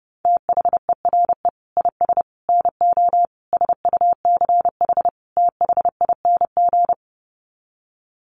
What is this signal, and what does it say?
Morse code, heard as a keyed tone, about 30 wpm, 700 Hz. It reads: THERE IS NO SUCH THING
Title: Sets of 5 Words Encore 30wpm